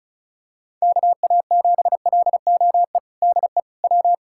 Morse code, heard as a keyed tone, about 34 wpm, 700 Hz. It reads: KA7LOE DE W
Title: USA Callsign DE Callsign - 6 Characters - 35wpm